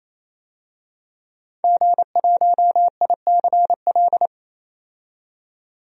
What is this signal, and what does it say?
Morse code, heard as a keyed tone, about 28 wpm, 700 Hz. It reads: G1ICL
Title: DXCC-100 Call Signs - 5 Characters - 28wpm